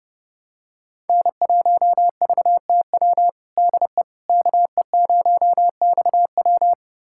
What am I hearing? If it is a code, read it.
Morse code, heard as a keyed tone, about 30 wpm, 700 Hz. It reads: N1VTW DE KE0XW
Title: USA Callsign DE Callsign - 5 Characters - 30wpm